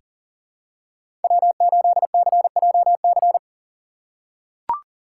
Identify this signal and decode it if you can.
Morse code, heard as a keyed tone, about 42 wpm, 690 Hz. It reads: W8CJC
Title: USA Call Signs - 5 Characters - 3x Character Spacing - 40wpm